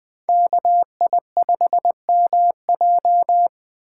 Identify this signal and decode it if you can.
Morse code, heard as a keyed tone, about 20 wpm, 700 Hz. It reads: KI5MJ
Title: Speed Racing - USA 2x2 Call Signs 17wpm